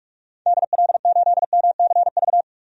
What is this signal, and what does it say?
Morse code, heard as a keyed tone, about 41 wpm, 700 Hz. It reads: DL8MCV
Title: DXCC-100 Call Signs - 6 Characters - 45wpm